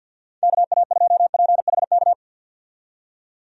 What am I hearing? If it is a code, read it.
Morse code, heard as a keyed tone, about 49 wpm, 700 Hz. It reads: KA2PHK
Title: USA Call Signs - 6 Characters - 50wpm